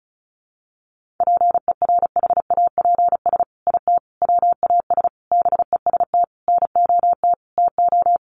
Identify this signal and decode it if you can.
Morse code, heard as a keyed tone, about 34 wpm, 700 Hz. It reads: PERHAPS IT WAS BEST NOT TO
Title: Sentences from Top 600 Words 35wpm